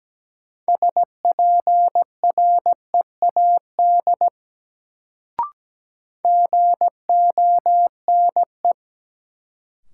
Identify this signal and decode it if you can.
Morse code, heard as a keyed tone, about 17 wpm, 700 Hz. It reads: SPREAD GONE
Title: Sentences from Top 1200 Words - New Word Drill - 17wpm